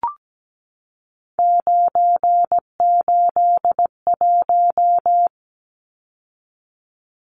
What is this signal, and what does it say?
Morse code, heard as a keyed tone, about 17 wpm, 700 Hz. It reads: 981
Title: Numbers - 3-Digit 17wpm